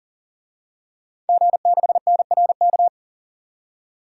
Morse code, heard as a keyed tone, about 40 wpm, 700 Hz. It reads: G6NRK
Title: DXCC-100 Call Signs - 5 Characters - 40wpm